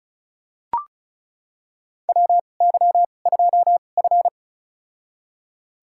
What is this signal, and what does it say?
Morse code, heard as a keyed tone, about 34 wpm, spaced as for 25 wpm, 700 Hz. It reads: WY2F
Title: USA Call Signs - 4 Characters - 2x Character Spacing - 35wpm